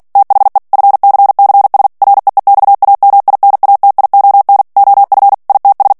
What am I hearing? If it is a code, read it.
Morse code, heard as a keyed tone, about 48 wpm, 800 Hz. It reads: THE FCCS REEXAMINATION OF ITS
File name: ANGLAIS texte clair 3 vitesse 45 mots minute